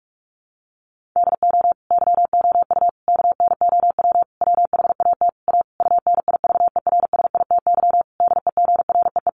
Morse code, heard as a keyed tone, about 44 wpm, 700 Hz. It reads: DO YOU KNOW WHAT A UNIVERSITY DEGREE
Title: Sentences from Top 1500 Words 45wpm